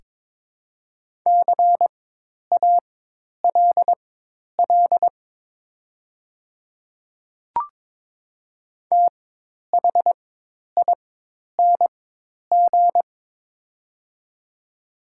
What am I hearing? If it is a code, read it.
Morse code, heard as a keyed tone, about 22 wpm, spaced as for 10 wpm, 690 Hz. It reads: CALL THING
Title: 4x Character Spacing - Top 200 Words - 22wpm